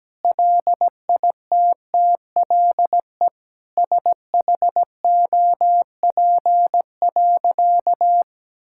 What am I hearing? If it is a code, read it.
Morse code, heard as a keyed tone, about 17 wpm, 700 Hz. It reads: LITTLE SHOP.
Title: Sentences from Top 1600 Words 17wpm